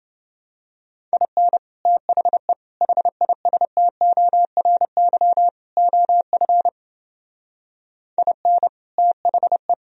Morse code, heard as a keyed tone, about 28 wpm, 700 Hz. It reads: IN THE HISTORY OF IN THE